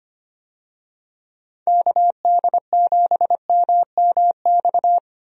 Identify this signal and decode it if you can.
Morse code, heard as a keyed tone, about 25 wpm, 700 Hz. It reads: KD7MMX
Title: USA Call Signs - 6 Characters - 25wpm